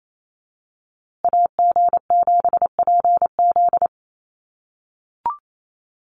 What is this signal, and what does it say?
Morse code, heard as a keyed tone, about 28 wpm, 700 Hz. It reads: AG7PZ
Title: USA Call Signs Repeated 2x 28wpm